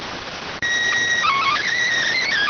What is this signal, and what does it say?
SSTV (slow-scan television), PD 290.